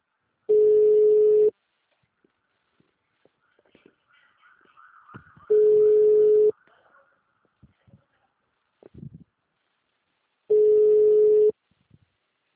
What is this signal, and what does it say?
Ringback tone (Europe, 425 Hz).